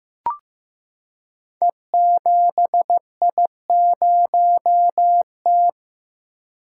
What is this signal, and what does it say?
Morse code, heard as a keyed tone, about 15 wpm, 700 Hz. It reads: E7I0T
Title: Code Groups - 5 Letter:Numbers - 15wpm